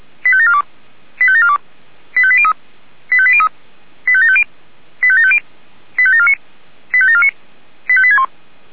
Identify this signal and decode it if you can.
Five-tone selective call (ZVEI), heard as quick runs of five tones, tones of about 70 ms. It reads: 85742 85742 85803 85803 85750 85750 85749 85749 85861